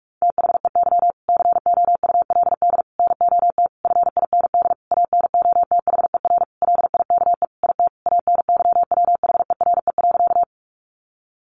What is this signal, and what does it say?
Morse code, heard as a keyed tone, about 44 wpm, 700 Hz. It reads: THEY COULD NOT FIND ANOTHER LIKE IT ANYWHERE.
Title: Sentences from Top 1400 Words 45wpm